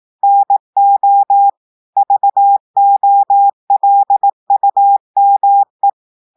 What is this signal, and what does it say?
Morse code, heard as a keyed tone, about 18 wpm, 800 Hz. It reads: NO VOLUME
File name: audios / portugais / portugais texte en clair 1 vitesse 18 mots minutes